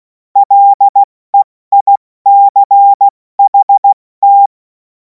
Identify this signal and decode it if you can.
Morse code, heard as a keyed tone, about 16 wpm, 800 Hz. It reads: LEICHT
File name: Morsekurs / Witze / Witz 24. März 2026 70 F80